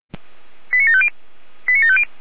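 Five-tone selective call (ZVEI), heard as quick runs of five tones, tones of about 70 ms. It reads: 89750 89750
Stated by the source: Five-tone ZVEI selective call received on SDR.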